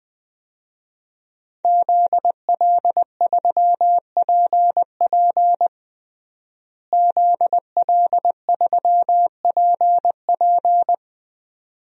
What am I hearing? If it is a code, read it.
Morse code, heard as a keyed tone, about 20 wpm, 700 Hz. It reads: ZL3PP ZL3PP
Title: DXCC-100 Call Signs - 5 Characters - Repeated 2x - 20wpm